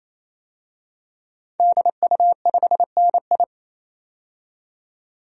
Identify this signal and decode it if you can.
Morse code, heard as a keyed tone, about 27 wpm, 700 Hz. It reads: DU5NI
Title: DXCC-100 Call Signs - 5 Characters - 28wpm